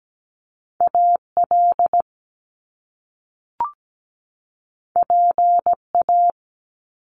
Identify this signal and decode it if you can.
Morse code, heard as a keyed tone, about 17 wpm, 700 Hz. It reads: AL PA
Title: Top Two Letter Combinations in Words 17wpm